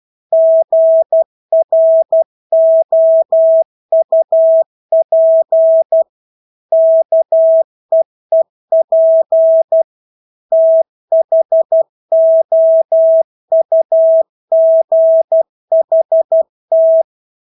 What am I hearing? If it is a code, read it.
Morse code, heard as a keyed tone, about 12 wpm, 650 Hz. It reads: GROUP KEEP THOUGHT
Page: Engelske ord 12 wpm